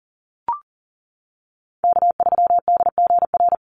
Morse code, heard as a keyed tone, about 40 wpm, 700 Hz. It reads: K3DGR
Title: Call Signs - 40wpm